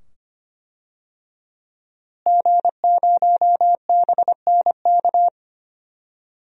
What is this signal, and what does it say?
Morse code, heard as a keyed tone, about 25 wpm, 700 Hz. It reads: G0BNK
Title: DXCC-100 Call Signs - 5 Characters - Repeated 2x - 25wpm